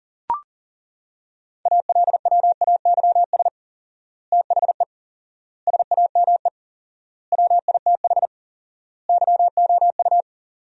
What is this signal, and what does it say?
Morse code, heard as a keyed tone, about 40 wpm, 700 Hz. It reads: ALWAYS THE SAME WITH YOU
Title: 4x Word-Spacing - Sentences from Top 300 Words - 40wpm